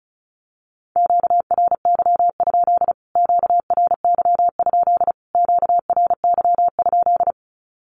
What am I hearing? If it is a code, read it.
Morse code, heard as a keyed tone, about 34 wpm, 700 Hz. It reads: QRY? QRY? QRY?
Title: Q-Codes Repeated 3x 35wpm